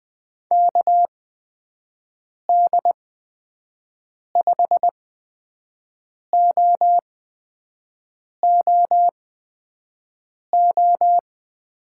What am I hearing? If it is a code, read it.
Morse code, heard as a keyed tone, about 20 wpm, spaced as for 5 wpm, 700 Hz. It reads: KD5OOO
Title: USA Call Signs - 6 Characters - 8x Character Spacing - 20wpm